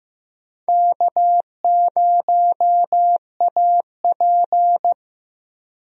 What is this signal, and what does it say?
Morse code, heard as a keyed tone, about 15 wpm, 700 Hz. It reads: K0AP
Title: USA Call Signs - 4 Characters - Repeated 2x - 15wpm